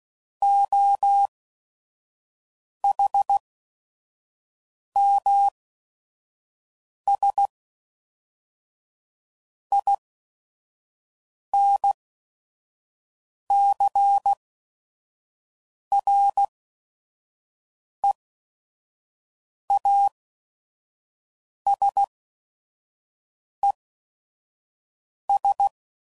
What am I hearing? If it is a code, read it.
Morse code, heard as a keyed tone, about 16 wpm, spaced as for 5 wpm, 800 Hz. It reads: OHMS INCREASES
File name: CW Code Groups @5WPM / Technical Subjects / XL